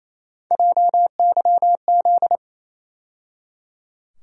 Morse code, heard as a keyed tone, about 28 wpm, 700 Hz. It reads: JYZ
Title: USA Call Sign Postfixes - 3 Letters 28wpm